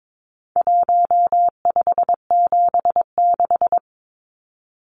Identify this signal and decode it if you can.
Morse code, heard as a keyed tone, about 22 wpm, 700 Hz. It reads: 1576
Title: Numbers - Contest Style - 4-Digit 22wpm